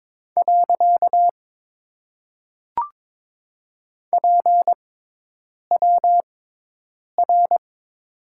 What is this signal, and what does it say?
Morse code, heard as a keyed tone, about 22 wpm, 700 Hz. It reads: . P W R
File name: Top 90 QSO Elements - 6x Character Spacing - 22wpm